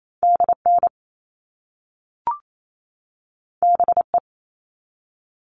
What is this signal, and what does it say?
Morse code, heard as a keyed tone, about 28 wpm, 700 Hz. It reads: DN BE
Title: All Cut Numbers - 2-Digit 28wpm